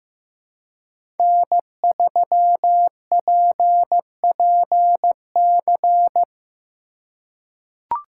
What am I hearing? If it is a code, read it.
Morse code, heard as a keyed tone, about 15 wpm, 700 Hz. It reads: N3PPC
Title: USA Call Signs - 5 Characters - 2x Character Spacing - 15wpm